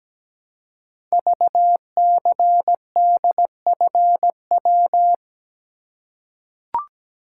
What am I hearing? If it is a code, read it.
Morse code, heard as a keyed tone, about 17 wpm, 700 Hz. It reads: VCDFW
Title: Code Groups - 5 Letters - 17wpm